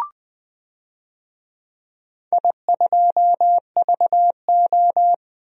Morse code, heard as a keyed tone, about 20 wpm, 700 Hz. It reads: I2VO
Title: DXCC-100 Call Signs - 4 Characters - Repeated 2x - 20wpm